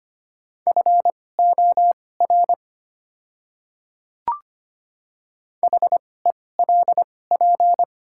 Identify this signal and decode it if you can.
Morse code, heard as a keyed tone, about 25 wpm, spaced as for 18 wpm, 700 Hz. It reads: FOR HELP
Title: 2x Character Spacing - Top 200 Words - 25wpm